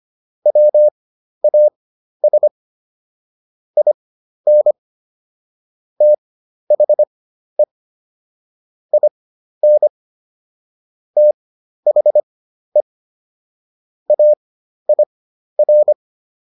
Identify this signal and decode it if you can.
Morse code, heard as a keyed tone, about 25 wpm, spaced as for 12 wpm, 600 Hz. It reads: WAS IN THE IN THE AIR